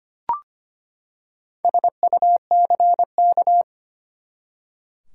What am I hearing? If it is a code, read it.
Morse code, heard as a keyed tone, about 25 wpm, 700 Hz. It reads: SUCK
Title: Four Letter Words - 25wpm